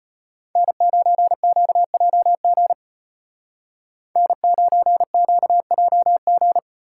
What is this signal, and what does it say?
Morse code, heard as a keyed tone, about 32 wpm, 700 Hz. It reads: N9QJG N9QJG